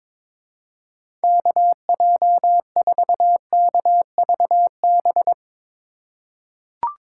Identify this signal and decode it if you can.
Morse code, heard as a keyed tone, about 22 wpm, 700 Hz. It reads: KJ4KVB
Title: USA Call Signs - 3x Character Spacing - 22wpm